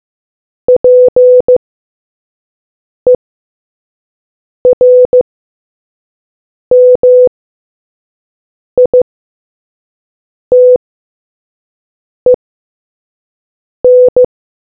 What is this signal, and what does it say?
Morse code, heard as a keyed tone, about 15 wpm, spaced as for 5 wpm, 500 Hz. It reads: PERMITEN